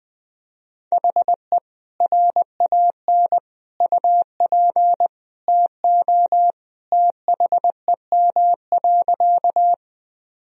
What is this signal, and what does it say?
Morse code, heard as a keyed tone, about 20 wpm, 700 Hz. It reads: HE RAN UP TO THEM.
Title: Sentences from Top 1000 Words - Review Part 3 - 20wpm